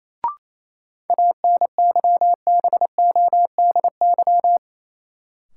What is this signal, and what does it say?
Morse code, heard as a keyed tone, about 27 wpm, 700 Hz. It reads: ANYBODY
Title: Sets of 5 Words - New Word Drill - 28wpm